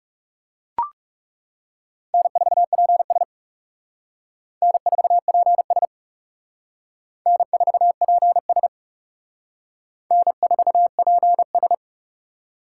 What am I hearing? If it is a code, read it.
Morse code, heard as a keyed tone, about 38 wpm, 700 Hz. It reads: N4PS N4PS N4PS N4PS